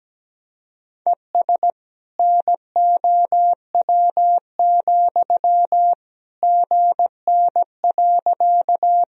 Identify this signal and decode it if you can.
Morse code, heard as a keyed tone, about 17 wpm, 700 Hz. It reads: ES NOW, GN.
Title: Sentences from Top 300 Words and Abbreviations 17wpm